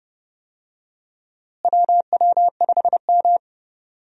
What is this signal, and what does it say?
Morse code, heard as a keyed tone, about 30 wpm, 700 Hz. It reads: WW5M
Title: USA Call Signs - 4 Characters - Repeated 2x - 30wpm